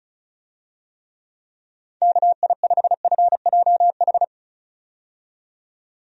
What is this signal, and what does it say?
Morse code, heard as a keyed tone, about 34 wpm, 700 Hz. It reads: KI5FJH
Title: USA Call Signs - 6 Characters - 35wpm